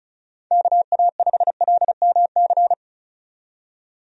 Morse code, heard as a keyed tone, about 36 wpm, 700 Hz. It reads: KA5LMC
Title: USA Call Signs - 6 Characters - 35wpm